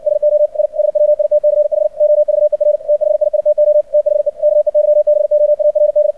Morse code, heard as a keyed tone, about 59 wpm, 600 Hz. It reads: SO I SWITCH OVER THIITO THE KEYBOARD
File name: archived / CRSnet